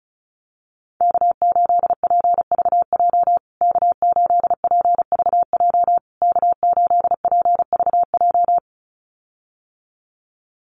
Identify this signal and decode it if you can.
Morse code, heard as a keyed tone, about 34 wpm, 700 Hz. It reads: K8PVJ K8PVJ K8PVJ
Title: USA Call Signs Repeated 3x 35wpm